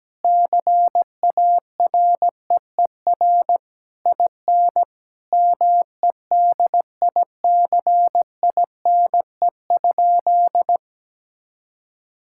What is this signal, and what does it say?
Morse code, heard as a keyed tone, about 17 wpm, 700 Hz. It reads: CAREER IN MEDICINE?
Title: Sentences from Top 1000 Words - Review Part 6 - 17wpm